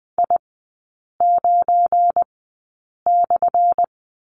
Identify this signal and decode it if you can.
Morse code, heard as a keyed tone, about 20 wpm, spaced as for 8 wpm, 700 Hz. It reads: I9/
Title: 178 Lesson 29e - Slash - Three Characters 20wpm